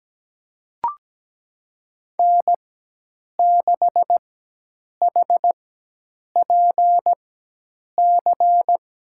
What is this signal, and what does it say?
Morse code, heard as a keyed tone, about 17 wpm, spaced as for 8 wpm, 700 Hz. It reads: N6HPC
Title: USA Call Signs - 4x Character Spacing - 17wpm